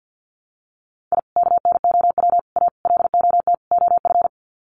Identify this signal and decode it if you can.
Morse code, heard as a keyed tone, about 50 wpm, 700 Hz. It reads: I KNOW A LOT OF
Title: Sets of 5 Words Encore 50wpm